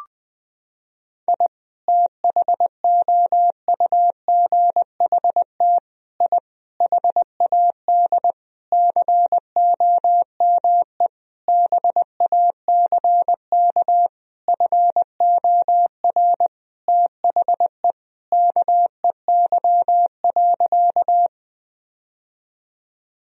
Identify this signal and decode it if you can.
Morse code, heard as a keyed tone, about 20 wpm, 700 Hz. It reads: I THOUGHT I HAD COME BACK FOR THE KEY.
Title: Sentences from Top 1000 Words - Review Part 4 - 20wpm